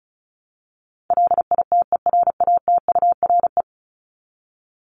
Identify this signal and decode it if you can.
Morse code, heard as a keyed tone, about 34 wpm, 700 Hz. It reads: LITERATURE
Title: Top 501-750 Words 35wpm